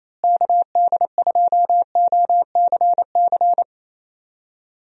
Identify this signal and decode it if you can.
Morse code, heard as a keyed tone, about 28 wpm, 700 Hz. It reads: KD2OCC
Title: USA Call Signs - 6 Characters - Repeated 2x - 28wpm